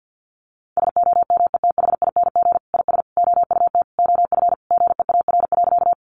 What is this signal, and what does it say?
Morse code, heard as a keyed tone, about 50 wpm, 700 Hz. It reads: SOMETHING IS OUT OF GEAR.
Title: Sentences from Top 1400 Words 50wpm